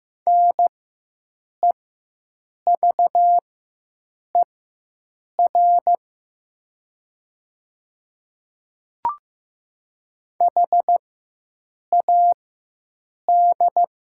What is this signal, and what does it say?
Morse code, heard as a keyed tone, about 15 wpm, spaced as for 7 wpm, 700 Hz. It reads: NEVER HAD
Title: 4x Character Spacing - Top 300 Words - 15wpm